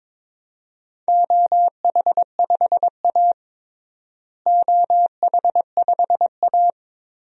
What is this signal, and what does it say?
Morse code, heard as a keyed tone, about 21 wpm, 700 Hz. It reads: OH5A OH5A
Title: DXCC-100 Call Signs - 4 Characters - Repeated 3x - 22wpm